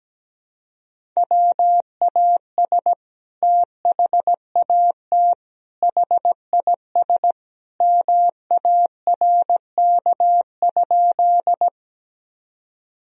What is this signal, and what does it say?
Morse code, heard as a keyed tone, about 17 wpm, 700 Hz. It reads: WAS THAT HIS MARK?
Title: Sentences from Top 1000 Words - Review Part 1 - 17wpm